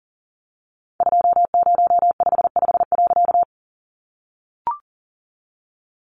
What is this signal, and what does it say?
Morse code, heard as a keyed tone, about 40 wpm, 700 Hz. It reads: 2055.
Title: A Sound Of Thunder 40wpm